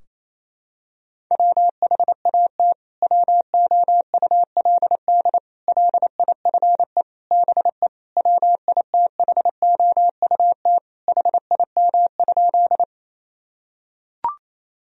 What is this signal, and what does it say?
Morse code, heard as a keyed tone, about 28 wpm, 700 Hz. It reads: WHAT WOULD LIFE BE WITHOUT HIM?
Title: Sentences from Top 1000 Words - Review Part 3 - 28wpm